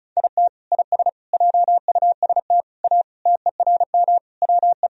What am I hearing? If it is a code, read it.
Morse code, heard as a keyed tone, about 34 wpm, 700 Hz. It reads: IT IS JUST A TERM WE
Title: Sentences from Top 1000 Words - Review Part 4 - 35wpm